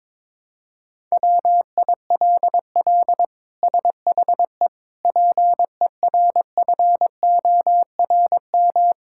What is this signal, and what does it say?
Morse code, heard as a keyed tone, about 21 wpm, 700 Hz. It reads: WILL SHE PERFORM
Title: Sentences from Top 1000 Words - Review Part 5 - 22wpm